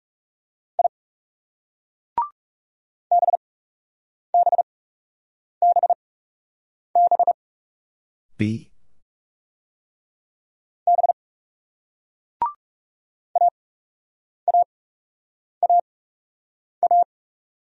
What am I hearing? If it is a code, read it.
Morse code, heard as a keyed tone, about 37 wpm, spaced as for 8 wpm, 700 Hz. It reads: I BBBB B AAAA
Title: Speed-Racing Single Letters 30wpm